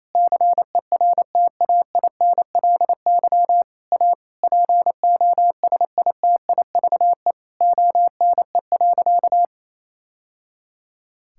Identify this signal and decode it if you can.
Morse code, heard as a keyed tone, about 28 wpm, 700 Hz. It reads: CERTAINLY A POSITIVE ONE.
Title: Sentences from Top 1000 Words - Review Part 6 - 28wpm